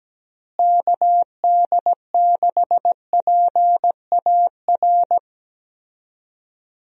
Morse code, heard as a keyed tone, about 17 wpm, 700 Hz. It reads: KD6PAR
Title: USA Call Signs - 6 Characters - Repeated 3x - 17wpm